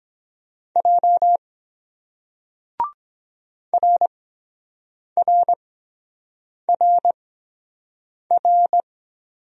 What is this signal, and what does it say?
Morse code, heard as a keyed tone, about 23 wpm, spaced as for 7 wpm, 700 Hz. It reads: J RRRR